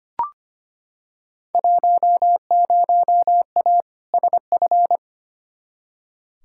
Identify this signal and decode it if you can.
Morse code, heard as a keyed tone, about 25 wpm, 700 Hz. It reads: 10A SF
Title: Field Day Exchanges Encore - 25wpm